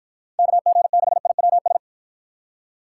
Morse code, heard as a keyed tone, about 52 wpm, 700 Hz. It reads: KG6IWS